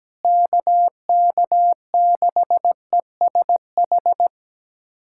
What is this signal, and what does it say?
Morse code, heard as a keyed tone, about 17 wpm, 700 Hz. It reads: KK6ESH
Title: USA Callsign DE Callsign 17wpm